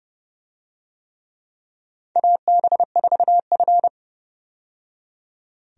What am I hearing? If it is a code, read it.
Morse code, heard as a keyed tone, about 30 wpm, 700 Hz. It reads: AB4F
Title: USA Call Signs - 4 Characters - 30wpm